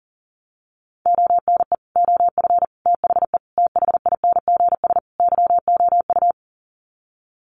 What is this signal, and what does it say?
Morse code, heard as a keyed tone, about 40 wpm, 700 Hz. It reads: ONE OF THE THINGS YOU
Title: Sets of 5 Words Encore2 40wpm